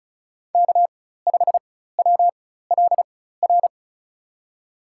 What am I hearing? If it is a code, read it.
Morse code, heard as a keyed tone, about 36 wpm, spaced as for 16 wpm, 700 Hz. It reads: K5WLR
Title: USA Call Signs - 5 Characters - 4x Character Spacing - 35wpm